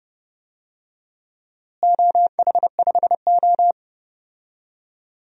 Morse code, heard as a keyed tone, about 30 wpm, 700 Hz. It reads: OH5O
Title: DXCC-100 Call Signs - 4 Characters - Repeated 2x - 30wpm